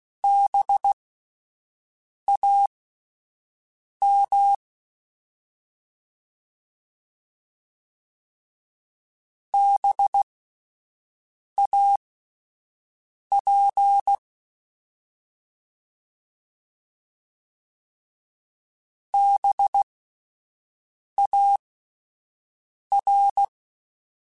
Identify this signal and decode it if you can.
Morse code, heard as a keyed tone, about 16 wpm, spaced as for 5 wpm, 800 Hz. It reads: BAM BAP BAR
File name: Spaced Alphas / Spaced Alphas mp3s / Bs sp @10